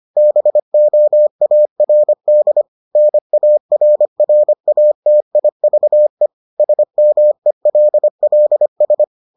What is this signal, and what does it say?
Morse code, heard as a keyed tone, about 25 wpm, 600 Hz. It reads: BOARD NARRATIVE SMELLS